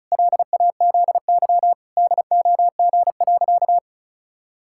Morse code, heard as a keyed tone, about 36 wpm, 700 Hz. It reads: LAZY DOG.
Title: Pangram Challenge v2 - 35wpm